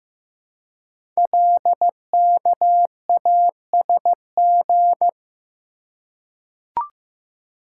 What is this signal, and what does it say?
Morse code, heard as a keyed tone, about 15 wpm, 700 Hz. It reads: LKASG
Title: Code Groups - 5 Letters - 15wpm